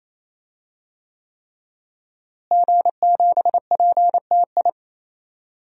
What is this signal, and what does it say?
Morse code, heard as a keyed tone, about 28 wpm, 700 Hz. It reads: G7PTI
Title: DXCC-100 Call Signs - 5 Characters - Repeated 3x - 28wpm